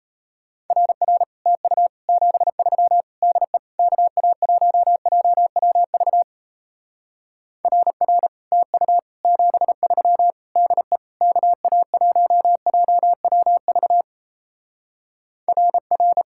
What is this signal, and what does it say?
Morse code, heard as a keyed tone, about 38 wpm, 700 Hz. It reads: RR TU 73 DE KA1JWV RR TU 73 DE KA1JWV RR